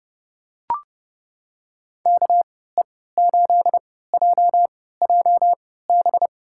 Morse code, heard as a keyed tone, about 32 wpm, spaced as for 17 wpm, 700 Hz. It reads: KE8JJB
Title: USA Call Signs - 6 Characters - 3x Character Spacing - 30wpm